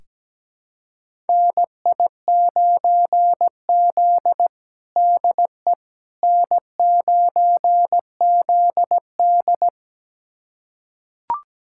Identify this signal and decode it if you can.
Morse code, heard as a keyed tone, about 17 wpm, 700 Hz. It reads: NI9Z DE N9ZD
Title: USA Callsign DE Callsign - 4 Characters - 17wpm